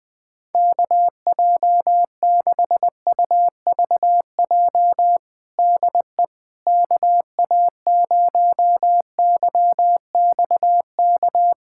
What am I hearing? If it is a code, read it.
Morse code, heard as a keyed tone, about 20 wpm, 700 Hz. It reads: KJ6UVJ DE KA0YXK
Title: USA Callsign DE Callsign - 6 Characters - 20wpm